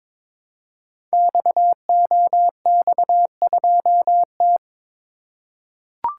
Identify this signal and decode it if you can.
Morse code, heard as a keyed tone, about 22 wpm, 700 Hz. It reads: XOX2T
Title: Code Groups - 5 Letter:Numbers - 22wpm